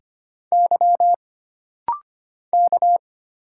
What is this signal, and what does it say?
Morse code, heard as a keyed tone, about 25 wpm, 700 Hz. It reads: Y K
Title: Single Letters - Rapid-Fire 25wpm